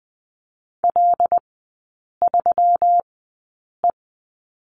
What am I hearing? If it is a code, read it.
Morse code, heard as a keyed tone, about 19 wpm, spaced as for 8 wpm, 700 Hz. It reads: L3E
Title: 133 Lesson 22e - Question Mark - Three Characters 20wpm